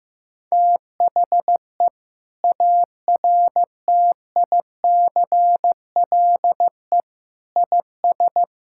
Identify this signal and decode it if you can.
Morse code, heard as a keyed tone, about 15 wpm, 700 Hz. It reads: THE ARTICLE IS
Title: Sentences from Top 1000 Words - Review Part 5 - 15wpm